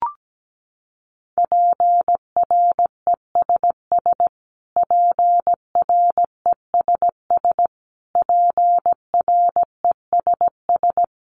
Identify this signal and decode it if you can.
Morse code, heard as a keyed tone, about 17 wpm, 700 Hz. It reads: PRESS PRESS PRESS
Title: Top 500 Words Repeated 17wpm